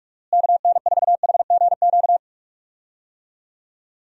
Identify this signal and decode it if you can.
Morse code, heard as a keyed tone, about 41 wpm, 700 Hz. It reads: KN4HGQ
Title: USA Call Signs - 6 Characters - Repeated 3x - 45wpm